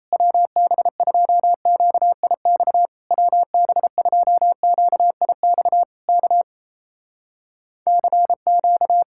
Morse code, heard as a keyed tone, about 27 wpm, 700 Hz. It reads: WB2QIX WB2QIX K CQ
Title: Speed-Racing Common QSO Phrases Encore - 28wpm